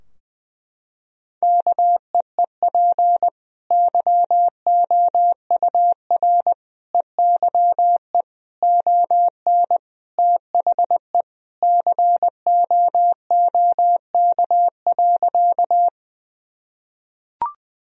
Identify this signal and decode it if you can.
Morse code, heard as a keyed tone, about 20 wpm, 700 Hz. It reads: KEEP YOUR EYE ON THE COOK.
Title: Sentences from Top 1000 Words - Review Part 4 - 20wpm